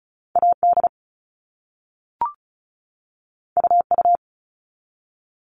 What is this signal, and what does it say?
Morse code, heard as a keyed tone, about 36 wpm, 700 Hz. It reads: AD UU
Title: All Cut Numbers - 2-Digit 35wpm